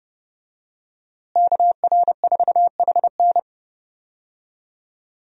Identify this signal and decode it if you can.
Morse code, heard as a keyed tone, about 30 wpm, 700 Hz. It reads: KR4HN
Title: USA Call Signs - 5 Characters - Repeated 3x - 30wpm